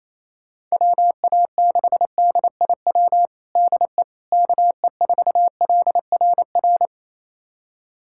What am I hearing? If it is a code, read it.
Morse code, heard as a keyed tone, about 28 wpm, 700 Hz. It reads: WA6DIW DE KE4LRR
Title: USA Callsign DE Callsign 28wpm